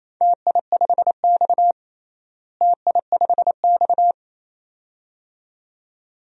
Morse code, heard as a keyed tone, about 28 wpm, 700 Hz. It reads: TI5X TI5X
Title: DXCC-100 Call Signs - 4 Characters - Repeated 2x - 28wpm